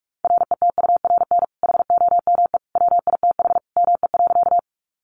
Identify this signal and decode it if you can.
Morse code, heard as a keyed tone, about 44 wpm, 700 Hz. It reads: RETURN HOME WITH ME.
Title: Sentences from Top 800 Words 45wpm